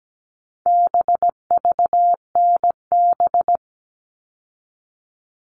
Morse code, heard as a keyed tone, about 17 wpm, 700 Hz. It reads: BVNB
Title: All Cut Numbers - 4-Digit 17wpm